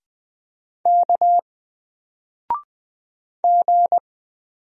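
Morse code, heard as a keyed tone, about 20 wpm, 700 Hz. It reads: K G
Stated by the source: Single letter and number Morse drill at 20 wpm, from N4LCD.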